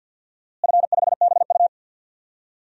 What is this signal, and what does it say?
Morse code, heard as a keyed tone, about 52 wpm, 700 Hz. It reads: F5BU
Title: DXCC-100 Call Signs - 4 Characters - Repeated 2x - 50wpm